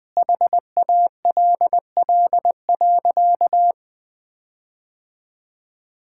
Morse code, heard as a keyed tone, about 20 wpm, 700 Hz. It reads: HALL.
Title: Sentences with 8-Word Phrases - 20wpm